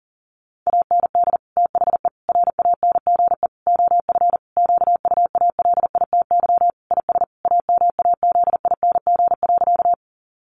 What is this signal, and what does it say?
Morse code, heard as a keyed tone, about 40 wpm, 700 Hz. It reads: AND THE RANGE OF QUALITY IS AMAZING.
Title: Sentences from Top 1500 Words 40wpm